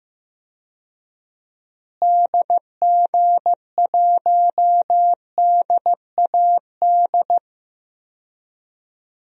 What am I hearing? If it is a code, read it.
Morse code, heard as a keyed tone, about 16 wpm, 700 Hz. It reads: DG1DAD
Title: DXCC-100 Call Signs - 6 Characters - Repeated 2x - 15wpm